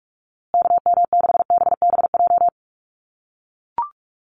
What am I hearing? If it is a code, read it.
Morse code, heard as a keyed tone, about 44 wpm, 700 Hz. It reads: KM6BBJ
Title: USA Call Signs Repeated 3x 45wpm